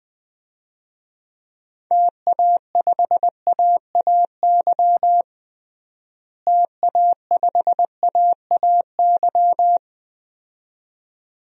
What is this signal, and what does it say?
Morse code, heard as a keyed tone, about 21 wpm, 700 Hz. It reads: TA5AAY TA5AAY
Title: DXCC-100 Call Signs - 6 Characters - Repeated 2x - 20wpm